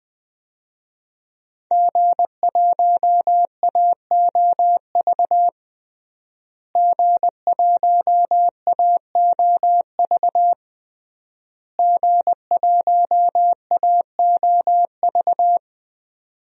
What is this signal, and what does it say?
Morse code, heard as a keyed tone, about 20 wpm, 700 Hz. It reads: G1AOV G1AOV G1AOV
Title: DXCC-100 Call Signs - 5 Characters - Repeated 3x - 20wpm